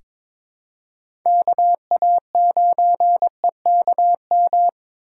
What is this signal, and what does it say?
Morse code, heard as a keyed tone, about 22 wpm, 700 Hz. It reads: KA9EKM
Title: USA Call Signs - 6 Characters - 2x Character Spacing - 22wpm